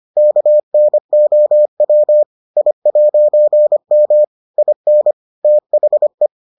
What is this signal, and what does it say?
Morse code, heard as a keyed tone, about 25 wpm, 600 Hz. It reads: KNOW I'M IN THE